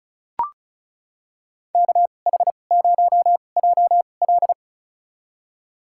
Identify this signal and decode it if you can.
Morse code, heard as a keyed tone, about 36 wpm, spaced as for 25 wpm, 700 Hz. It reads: KH0JL
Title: USA Call Signs - 5 Characters - 2x Character Spacing - 35wpm